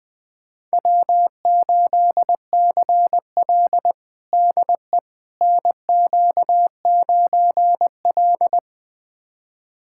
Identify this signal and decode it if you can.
Morse code, heard as a keyed tone, about 20 wpm, 700 Hz. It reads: W8CL DE NQ9L
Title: USA Callsign DE Callsign - 4 Characters - 20wpm